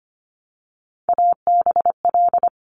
Morse code, heard as a keyed tone, about 25 wpm, 700 Hz. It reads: ABL
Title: Top Three Letter Combinations in Words 25wpm